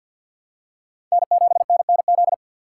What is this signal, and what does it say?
Morse code, heard as a keyed tone, about 53 wpm, 700 Hz. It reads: N7NNZ
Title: USA Call Signs - 6x Character Spacing - 50wpm